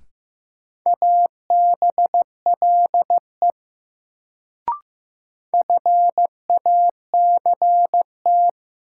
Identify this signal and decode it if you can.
Morse code, heard as a keyed tone, about 15 wpm, 700 Hz. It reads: ABLE FACT
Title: Sets of 2 Words - New Word Drill - 15wpm